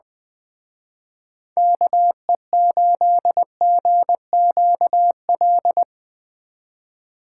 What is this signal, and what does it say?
Morse code, heard as a keyed tone, about 22 wpm, 700 Hz. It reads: KE8GQL
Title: Speed Racing - USA Call Signs 20wpm